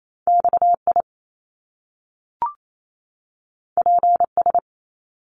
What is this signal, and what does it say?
Morse code, heard as a keyed tone, about 28 wpm, 700 Hz. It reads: XI PS
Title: Top Two Letter Combinations in Words 28wpm